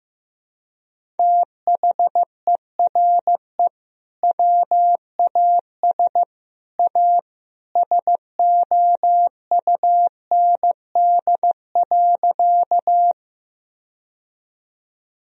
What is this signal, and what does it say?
Morse code, heard as a keyed tone, about 15 wpm, 700 Hz. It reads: THERE WAS A SOUND.
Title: Sentences from Top 1000 Words - Review Part 1 - 15wpm